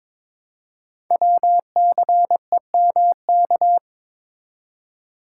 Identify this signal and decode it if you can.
Morse code, heard as a keyed tone, about 22 wpm, 700 Hz. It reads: WCEMK
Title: Code Groups - 5 Letters - 22wpm